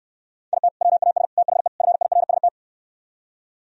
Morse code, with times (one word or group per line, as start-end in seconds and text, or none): 0.5-0.7 IT
0.8-1.2 WAS
1.4-1.7 THE
1.8-2.5 FIRST